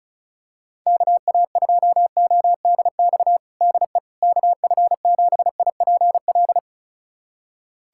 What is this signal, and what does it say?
Morse code, heard as a keyed tone, about 36 wpm, 700 Hz. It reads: KA2ODX DE KF7IPL
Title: USA Callsign DE Callsign - 6 Characters - 35wpm